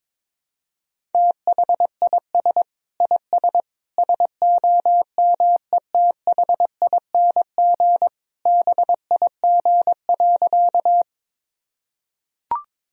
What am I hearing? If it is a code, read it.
Morse code, heard as a keyed tone, about 22 wpm, 700 Hz. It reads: THIS IS SOMETHING BIG.
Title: Sentences from Top 1000 Words - Review Part 1 - 22wpm